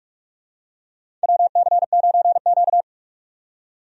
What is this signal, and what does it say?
Morse code, heard as a keyed tone, about 43 wpm, 700 Hz. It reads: WC9Q
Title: USA Call Signs - 4 Characters - 45wpm